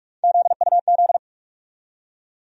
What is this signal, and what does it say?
Morse code, heard as a keyed tone, about 45 wpm, 700 Hz. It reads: ZUZ
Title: USA Call Sign Postfixes - 3 Letters 45wpm